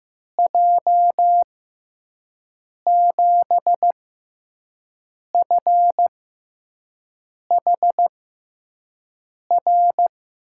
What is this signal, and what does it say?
Morse code, heard as a keyed tone, about 15 wpm, spaced as for 5 wpm, 700 Hz. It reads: J7FHR
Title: USA Call Signs - 6 Characters - 6x Character Spacing - 15wpm